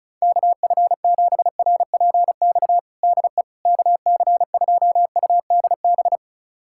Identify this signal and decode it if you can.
Morse code, heard as a keyed tone, about 36 wpm, 700 Hz. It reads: KF7RPX DE KC2UDB
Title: USA Callsign DE Callsign - 6 Characters - 35wpm